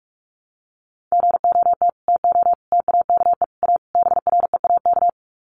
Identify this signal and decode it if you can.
Morse code, heard as a keyed tone, about 44 wpm, 700 Hz. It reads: GOT TO TAKE A BREAK
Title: Sets of 5 Words Encore 45wpm